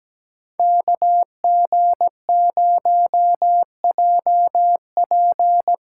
Morse code, heard as a keyed tone, about 20 wpm, 700 Hz. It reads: KG0JP
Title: Speed Racing - USA 2x2 Call Signs 17wpm